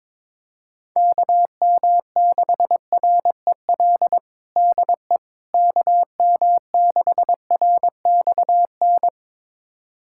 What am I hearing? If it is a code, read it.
Morse code, heard as a keyed tone, about 21 wpm, 700 Hz. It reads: KM6REL DE KM6RXN
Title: USA Callsign DE Callsign - 6 Characters - 22wpm